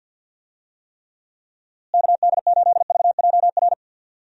Morse code, heard as a keyed tone, about 51 wpm, 700 Hz. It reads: KD8VJR
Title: USA Call Signs - 6 Characters - Repeated 2x - 50wpm